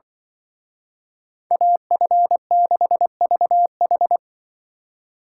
Morse code, heard as a keyed tone, about 24 wpm, 700 Hz. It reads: AF6VH
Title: Speed Racing - USA 2x2 Call Signs 20wpm